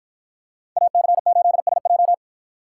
Morse code, heard as a keyed tone, about 50 wpm, 700 Hz. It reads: AC8SJ